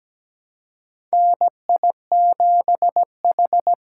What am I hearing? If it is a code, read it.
Morse code, heard as a keyed tone, about 17 wpm, 700 Hz. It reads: NI7H
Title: USA Call Signs - 4 Characters - 3x Character Spacing - 17wpm